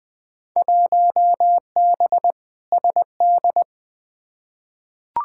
Morse code, heard as a keyed tone, about 20 wpm, 700 Hz. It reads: 1B SD
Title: Field Day Exchanges Encore - 20wpm